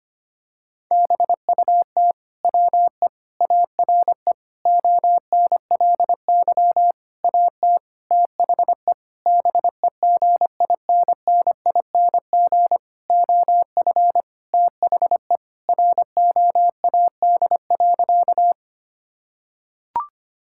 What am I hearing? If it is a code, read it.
Morse code, heard as a keyed tone, about 25 wpm, 700 Hz. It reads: BUT WE ARE ONLY AT THE BEGINNING OF THE ROAD.
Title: Sentences from Top 1000 Words - Review Part 5 - 25wpm